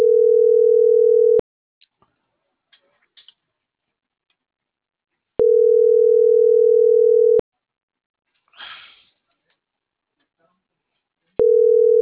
Ringback tone (North America), 440 + 480 Hz.